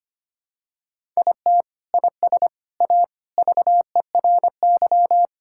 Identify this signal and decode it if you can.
Morse code, heard as a keyed tone, about 25 wpm, 700 Hz. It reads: IT IS A VERY
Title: Sentences from Top 1600 Words 25wpm